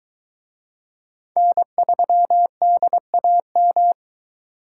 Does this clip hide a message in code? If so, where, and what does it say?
Yes: Morse code from 1.4 to 3.9 s, heard as a keyed tone, about 20 wpm, 700 Hz. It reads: N3DAM